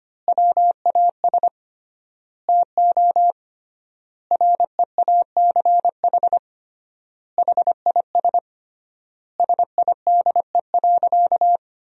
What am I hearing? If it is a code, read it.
Morse code, heard as a keyed tone, about 25 wpm, 700 Hz. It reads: WAS TO REACH HIS SIDE.
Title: 3x Word-Spacing - Sentences from Top 400 Words - 25wpm